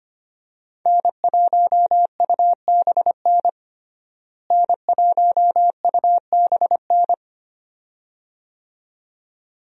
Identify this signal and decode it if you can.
Morse code, heard as a keyed tone, about 25 wpm, 700 Hz. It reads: N1UBN N1UBN
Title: USA Call Signs - 5 Characters - Repeated 2x - 25wpm